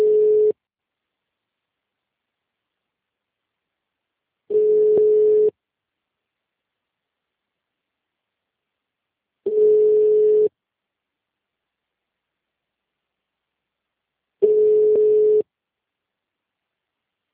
Ringback tone (Europe, 425 Hz).
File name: out- -2011- - - .28844